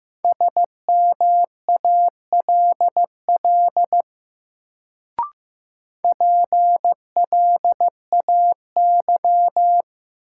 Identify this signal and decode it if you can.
Morse code, heard as a keyed tone, about 15 wpm, 700 Hz. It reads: SMALL PLAY
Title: Sentences from Top 200 Words - New Word Drill - 15wpm